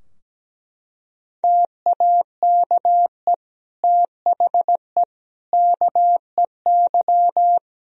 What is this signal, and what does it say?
Morse code, heard as a keyed tone, about 17 wpm, 700 Hz. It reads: TAKE THE KEY
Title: Sentences from Top 1000 Words - Review Part 4 - 17wpm